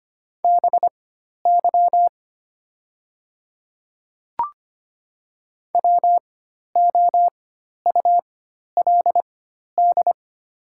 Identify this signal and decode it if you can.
Morse code, heard as a keyed tone, about 25 wpm, spaced as for 12 wpm, 700 Hz. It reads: BY WOULD
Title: 4x Character Spacing - Top 100 Words - 25wpm